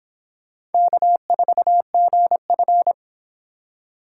Morse code, heard as a keyed tone, about 30 wpm, 700 Hz. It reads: K4GF